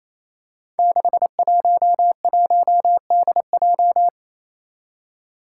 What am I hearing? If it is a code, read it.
Morse code, heard as a keyed tone, about 28 wpm, 700 Hz. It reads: 611DJ
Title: Five Challenging Letter-Number Combinations 28wpm